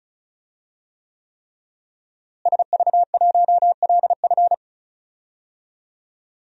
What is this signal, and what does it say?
Morse code, heard as a keyed tone, about 36 wpm, 700 Hz. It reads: SV1LF
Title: DXCC-100 Call Signs - 5 Characters - Repeated 3x - 35wpm